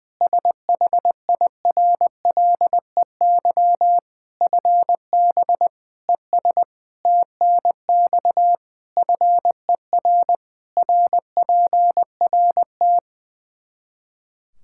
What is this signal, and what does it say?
Morse code, heard as a keyed tone, about 20 wpm, 700 Hz. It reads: SHIRLEY FB ES TNX FER RPRT
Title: Practice QSO - Exchange 2 - Info - 20wpm